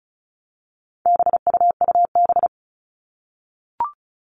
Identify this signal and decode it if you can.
Morse code, heard as a keyed tone, about 34 wpm, 700 Hz. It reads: BUUB
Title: All Cut Numbers - 4-Digit 35wpm